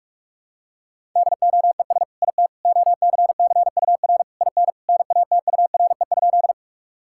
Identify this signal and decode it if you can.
Morse code, heard as a keyed tone, about 46 wpm, 700 Hz. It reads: DOES IT OCCUR IN NATURE?
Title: Sentences from Top 1000 Words - Review Part 6 - 45wpm